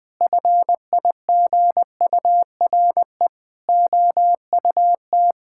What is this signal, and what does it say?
Morse code, heard as a keyed tone, about 20 wpm, 700 Hz. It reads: FIGURE OUT
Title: Sentences with 6-Word Phrases - 20wpm